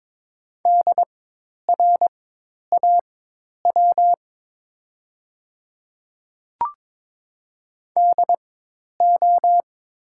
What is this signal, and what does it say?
Morse code, heard as a keyed tone, about 22 wpm, spaced as for 10 wpm, 700 Hz. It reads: DRAW DO
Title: 4x Character Spacing - Top 300 Words - 22wpm